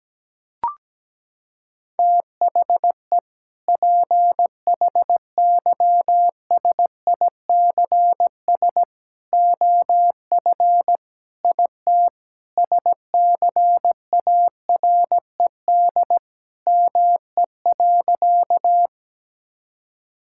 Morse code, heard as a keyed tone, about 17 wpm, 700 Hz. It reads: THE PHYSICS OF IT SCARED ME.
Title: Sentences from Top 1000 Words - Review Part 5 - 17wpm